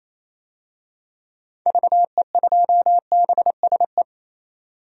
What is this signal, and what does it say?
Morse code, heard as a keyed tone, about 28 wpm, 700 Hz. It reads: VE2BSE
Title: DXCC-100 Call Signs - 6 Characters - 28wpm